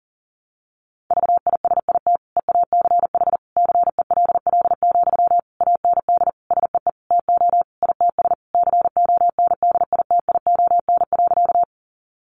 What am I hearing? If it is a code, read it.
Morse code, heard as a keyed tone, about 40 wpm, 700 Hz. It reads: VISIT EACH CELL, AND SEE TO ITS CONDITION.
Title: Sentences from Top 1100 Words 40wpm